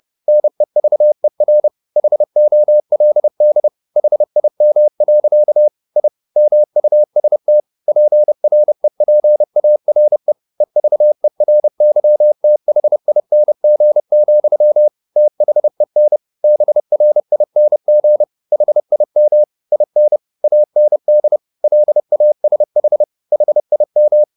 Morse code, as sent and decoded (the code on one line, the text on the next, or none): -. ...- . .-. / .... --- .-.. -.. / .... .. -- .-.-.- / .. / -- ..- ... - / .--. .-. .--. .- .-. / . ...- . .-. -.-- - .... .. -. --. --..-- / - .... -. / -... .-. .. -. --. / .... .. -- / .. -. / .- -. -.. / .-.. .- ... .... / .... .. --
NEVER HOLD HIM. I MUST PREPARE EVERYTHING, THEN BRING HIM IN AND LASH HIM